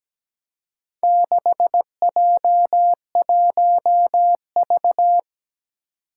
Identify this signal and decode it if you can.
Morse code, heard as a keyed tone, about 17 wpm, 700 Hz. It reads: 6J1V
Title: Four Challenging Letter-Number Combinations 17wpm